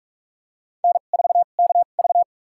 Morse code, heard as a keyed tone, about 46 wpm, spaced as for 33 wpm, 700 Hz. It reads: N4KV
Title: USA Call Signs - 4 Characters - 2x Character Spacing - 45wpm